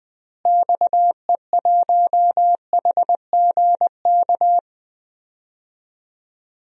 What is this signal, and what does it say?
Morse code, heard as a keyed tone, about 19 wpm, 700 Hz. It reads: XE1HGK
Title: DXCC-100 Call Signs - 6 Characters - Repeated 3x - 20wpm